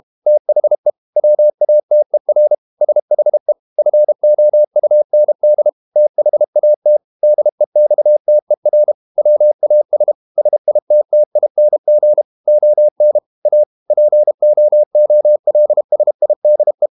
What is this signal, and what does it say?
Morse code, heard as a keyed tone, about 32 wpm, 600 Hz. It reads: THE WATER SHE FOUND THAT DEXTER WAS SITTING ON A POOLSIDE
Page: Never 31wpm | CW med Gnister